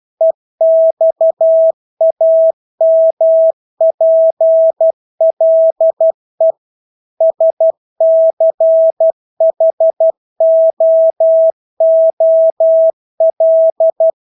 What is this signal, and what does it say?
Morse code, heard as a keyed tone, about 12 wpm, 650 Hz. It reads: EXAMPLE SCHOOL
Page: Engelske ord 12 wpm